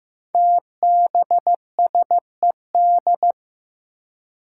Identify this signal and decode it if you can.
Morse code, heard as a keyed tone, about 15 wpm, 700 Hz. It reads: TBSED
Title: Five Challenging Letter-Number Combinations 15wpm